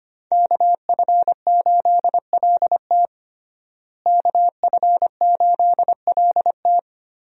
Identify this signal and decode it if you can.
Morse code, heard as a keyed tone, about 25 wpm, 700 Hz. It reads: KF8LT KF8LT
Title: USA Call Signs - 5 Characters - Repeated 3x - 25wpm